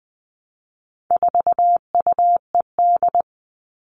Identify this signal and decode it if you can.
Morse code, heard as a keyed tone, about 20 wpm, 700 Hz. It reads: 4UED
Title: All Cut Numbers - 4-Digit 20wpm